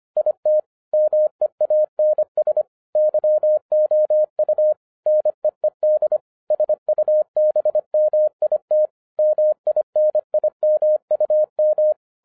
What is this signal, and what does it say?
Morse code, heard as a keyed tone, about 26 wpm, 610 Hz. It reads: IT MEANS YOU NEED SUBMIT MINIMUM